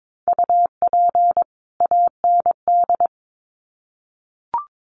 Morse code, heard as a keyed tone, about 22 wpm, 700 Hz. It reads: UP AND
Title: Sets of 2 Words 22wpm